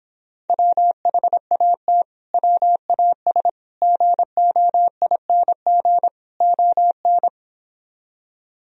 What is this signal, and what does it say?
Morse code, heard as a keyed tone, about 26 wpm, 700 Hz. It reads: WHAT WAS GOING ON